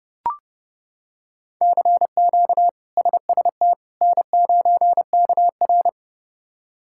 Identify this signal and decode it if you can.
Morse code, heard as a keyed tone, about 30 wpm, 700 Hz. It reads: CQ SST N9KR
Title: K1USN SST QSO Practice - 30wpm